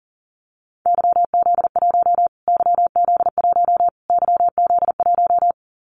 Morse code, heard as a keyed tone, about 40 wpm, 700 Hz. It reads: YZ1 YZ1 YZ1
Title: Callsign Prefixes for Top 25 DXCC - Repeated 3x - 40wpm